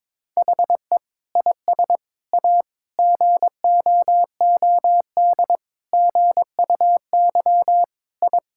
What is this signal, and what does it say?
Morse code, heard as a keyed tone, about 22 wpm, 700 Hz. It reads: HE IS A GOOD GUY I
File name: practice / 22wpm / Sets of 8 Words - 22wpm